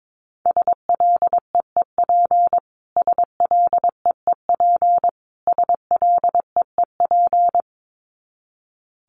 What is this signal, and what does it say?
Morse code, heard as a keyed tone, about 22 wpm, 700 Hz. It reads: SLEEP SLEEP SLEEP
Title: Top 500 Words Repeated 22wpm